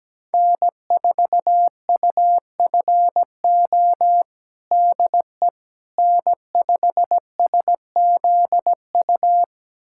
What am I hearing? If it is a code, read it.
Morse code, heard as a keyed tone, about 17 wpm, 700 Hz. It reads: N4UFO DE N5SZU
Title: USA Callsign DE Callsign - 5 Characters - 17wpm